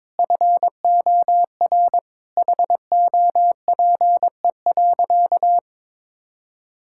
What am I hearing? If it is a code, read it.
Morse code, heard as a keyed tone, about 22 wpm, 700 Hz. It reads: FOR HOPE.
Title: Sentences from Top 1000 Words - Review Part 5 - 22wpm